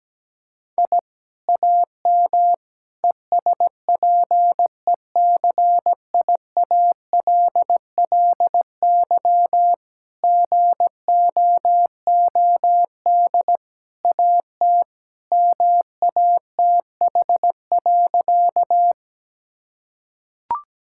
Morse code, heard as a keyed tone, about 17 wpm, 700 Hz. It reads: I AM ESPECIALLY GOOD AT MATH.
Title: Sentences from Top 1000 Words - Review Part 6 - 17wpm